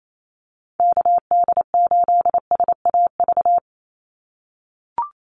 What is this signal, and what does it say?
Morse code, heard as a keyed tone, about 28 wpm, 700 Hz. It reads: KD8SAV
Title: Call Signs - 28wpm